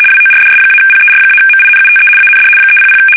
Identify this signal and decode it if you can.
RTTY, about 75 baud. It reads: WORDS-PER-MINUTE AND 300
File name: fsk850-75B / teletype-corp